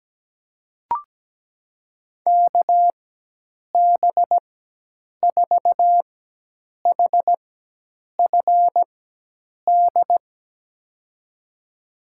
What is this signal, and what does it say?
Morse code, heard as a keyed tone, about 17 wpm, spaced as for 8 wpm, 700 Hz. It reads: KB4HFD
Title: USA Call Signs - 6 Characters - 4x Character Spacing - 17wpm